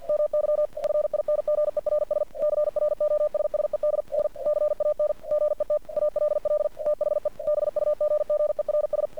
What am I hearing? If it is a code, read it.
Morse code, heard as a keyed tone, about 48 wpm, 600 Hz. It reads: MY FINGERS CROSSED N CAN GET ALL THE BUGGERS
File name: archived / CRSnet